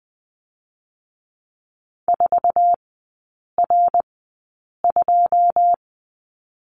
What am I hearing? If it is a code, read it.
Morse code, heard as a keyed tone, about 20 wpm, spaced as for 8 wpm, 690 Hz. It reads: 4R2
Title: 087 Lesson 15e - Number 5 - Three Characters 20wpm